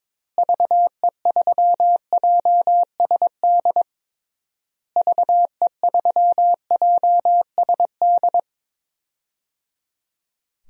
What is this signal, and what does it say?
Morse code, heard as a keyed tone, about 22 wpm, 700 Hz. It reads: VE3JSD VE3JSD
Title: DXCC-100 Call Signs - 6 Characters - Repeated 3x - 22wpm